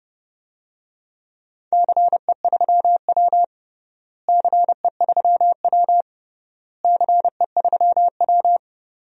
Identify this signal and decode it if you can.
Morse code, heard as a keyed tone, about 30 wpm, 700 Hz. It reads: CE3W CE3W CE3W
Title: DXCC-100 Call Signs - 4 Characters - Repeated 3x - 30wpm